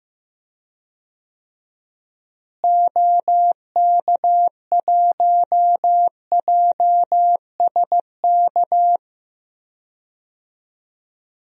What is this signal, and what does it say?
Morse code, heard as a keyed tone, about 15 wpm, 700 Hz. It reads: OK1JSK
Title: DXCC-100 Call Signs - 6 Characters - Repeated 2x - 15wpm